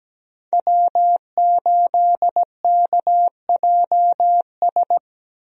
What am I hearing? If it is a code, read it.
Morse code, heard as a keyed tone, about 17 wpm, 700 Hz. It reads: W8KJS
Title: USA Call Signs - 5 Characters - Repeated 3x - 17wpm